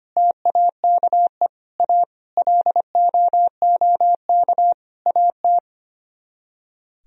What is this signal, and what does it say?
Morse code, heard as a keyed tone, about 32 wpm, 700 Hz. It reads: TAKE A LOOK AT